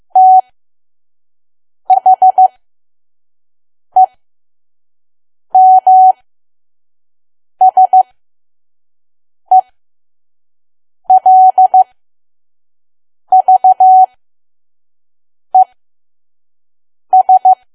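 Morse code, heard as a keyed tone, about 15 wpm, spaced as for 5 wpm, 750 Hz. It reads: THEMSELVES
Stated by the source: Morse code practice at 5 WPM from KQ2P.COM.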